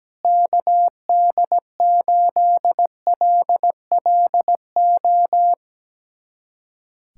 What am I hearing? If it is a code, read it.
Morse code, heard as a keyed tone, about 17 wpm, 700 Hz. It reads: KD8LLO
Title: USA Callsign DE Callsign - 6 Characters - 17wpm